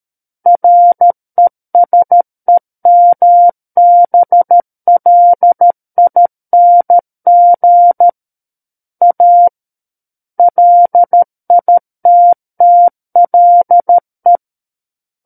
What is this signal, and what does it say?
Morse code, heard as a keyed tone, about 13 wpm, 700 Hz. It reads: RESEMBLING A LITTLE
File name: Book One / 04-Chapter 4 - 13 WPM